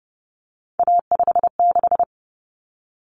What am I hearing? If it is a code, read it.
Morse code, heard as a keyed tone, about 30 wpm, 700 Hz. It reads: A56
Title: Common Cut Numbers - 3-Digit 30wpm